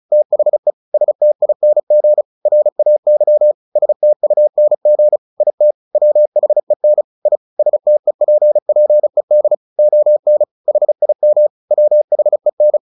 Morse code, heard as a keyed tone, about 34 wpm, 600 Hz. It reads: THE STING RAY STUNG IT WHEN I STEPPED ON HIM WHEN
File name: the old man and the sea 2 35wpm